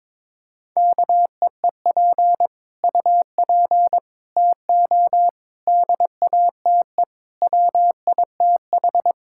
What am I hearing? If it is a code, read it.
Morse code, heard as a keyed tone, about 22 wpm, 700 Hz. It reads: KEEP UP TO DATE WITH
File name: Sets of 9 Words - 22wpm